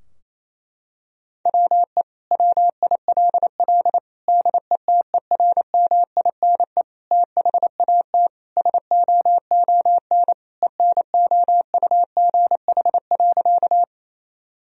Morse code, heard as a keyed tone, about 28 wpm, 700 Hz. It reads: WE WILL DETERMINE THAT SOON ENOUGH.
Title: Sentences from Top 1000 Words - Review Part 4 - 28wpm